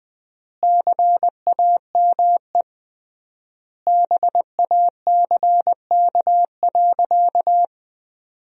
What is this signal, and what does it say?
Morse code, heard as a keyed tone, about 20 wpm, 700 Hz. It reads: CAME BACK.
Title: 3x Word-Spacing - Sentences from Top 300 Words - 20wpm